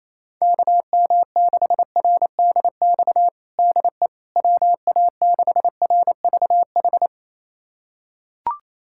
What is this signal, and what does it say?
Morse code, heard as a keyed tone, about 28 wpm, 700 Hz. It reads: KM6RDX DE WA6RVH
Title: USA Callsign DE Callsign - 6 Characters - 28wpm